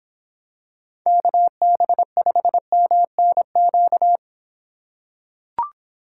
Morse code, heard as a keyed tone, about 24 wpm, 700 Hz. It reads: KB5MNQ